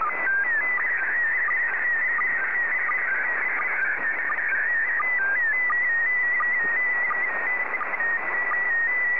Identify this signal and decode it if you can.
SSTV (slow-scan television), PD 90.
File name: sstv / PD90-ZL2TJM